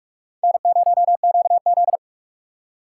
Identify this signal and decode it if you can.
Morse code, heard as a keyed tone, about 41 wpm, 700 Hz. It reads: N0QZ
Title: USA Call Signs - 4 Characters - 4x Character Spacing - 45wpm